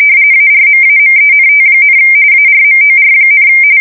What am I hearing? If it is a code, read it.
RTTY, about 45.45 baud. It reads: ARMAMENT, ARMOR,
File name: fsk170 / USS-New-Jersey